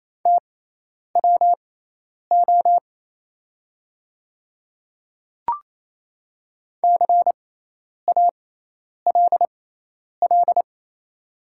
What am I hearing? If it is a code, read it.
Morse code, heard as a keyed tone, about 28 wpm, spaced as for 10 wpm, 700 Hz. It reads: TWO CALL
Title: 6x Character Spacing - Top 200 Words - 28wpm